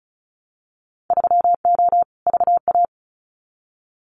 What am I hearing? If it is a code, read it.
Morse code, heard as a keyed tone, about 34 wpm, 700 Hz. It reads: 3O VA
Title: Winter Field Day Exchanges 35wpm